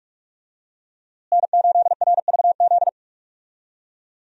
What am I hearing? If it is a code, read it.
Morse code, heard as a keyed tone, about 42 wpm, 700 Hz. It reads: N8RVZ
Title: USA Call Signs - 5 Characters - Repeated 2x - 45wpm